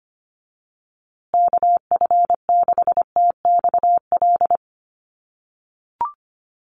Morse code, heard as a keyed tone, about 25 wpm, 700 Hz. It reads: KF6TXL
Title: USA Call Signs Repeated 3x 25wpm